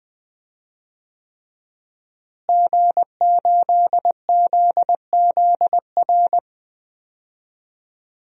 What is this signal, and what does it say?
Morse code, heard as a keyed tone, about 20 wpm, 700 Hz. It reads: G8ZZR
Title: DXCC-100 Call Signs - 5 Characters - Repeated 3x - 20wpm